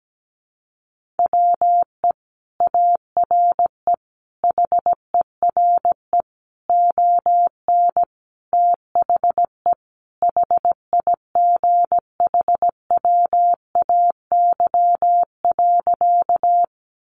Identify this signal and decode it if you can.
Morse code, heard as a keyed tone, about 17 wpm, 700 Hz. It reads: WE ARE HERE ON THE HIGHWAY.
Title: Sentences from Top 1400 Words 17wpm